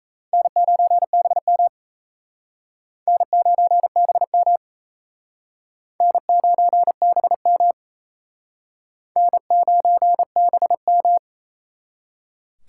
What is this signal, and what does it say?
Morse code, heard as a keyed tone, about 37 wpm, 700 Hz. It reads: N9BM N9BM N9BM N9BM